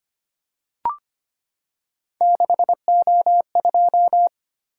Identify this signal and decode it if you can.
Morse code, heard as a keyed tone, about 25 wpm, 700 Hz. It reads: 6O2
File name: Three Letter-Number Combinations - 25wpm